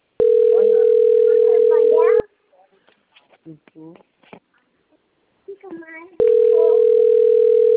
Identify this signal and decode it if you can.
Ringback tone (North America), 440 + 480 Hz, with speech.
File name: rg-1600-unknown- - - .73683